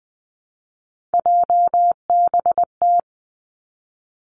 Morse code, heard as a keyed tone, about 20 wpm, 700 Hz. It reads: JBT
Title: Three Challenging Letter-Number Combinations 20wpm